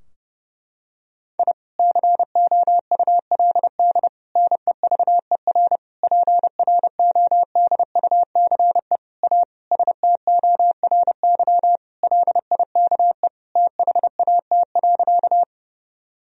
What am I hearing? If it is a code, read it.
Morse code, heard as a keyed tone, about 30 wpm, 700 Hz. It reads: I COULD NEVER PRODUCE A STORY LIKE THAT.
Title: Sentences from Top 1000 Words - Review Part 3 - 30wpm